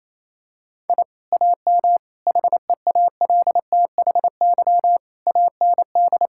Morse code, heard as a keyed tone, about 27 wpm, 700 Hz. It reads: I AM HEALTHY AND
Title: Sentences from Top 1000 Words - Review Part 5 - 28wpm